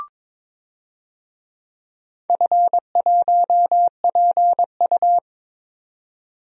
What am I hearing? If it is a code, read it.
Morse code, heard as a keyed tone, about 22 wpm, 700 Hz. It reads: F1PU
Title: DXCC-100 Call Signs - 4 Characters - 22wpm